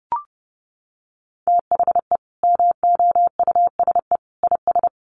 Morse code, heard as a keyed tone, about 30 wpm, 700 Hz. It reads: THE MOUSE IS
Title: Easy Phrases 30wpm